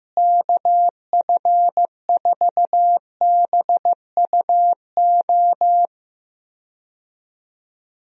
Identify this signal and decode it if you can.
Morse code, heard as a keyed tone, about 15 wpm, 700 Hz. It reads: KF4BUO
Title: USA Call Signs - 6 Characters - Repeated 3x - 15wpm